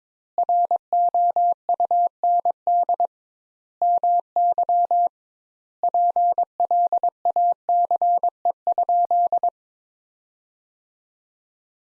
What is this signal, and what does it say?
Morse code, heard as a keyed tone, about 22 wpm, 700 Hz. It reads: ROUND MY PLACE?
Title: Sentences Top 200 Words - 22wpm